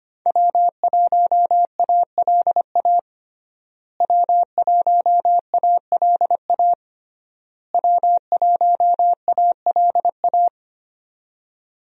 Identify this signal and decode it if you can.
Morse code, heard as a keyed tone, about 25 wpm, 700 Hz. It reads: W1ALA W1ALA W1ALA
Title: USA Call Signs - 5 Characters - Repeated 3x - 25wpm